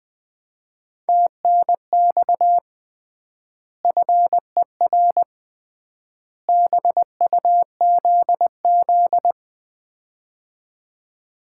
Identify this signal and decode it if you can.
Morse code, heard as a keyed tone, about 21 wpm, 700 Hz. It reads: TNX FER BUZZ
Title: Common QSO Phrases - 3x Word-Spacing - 20wpm